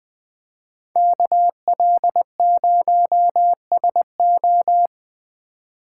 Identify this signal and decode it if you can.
Morse code, heard as a keyed tone, about 20 wpm, 700 Hz. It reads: KL0SO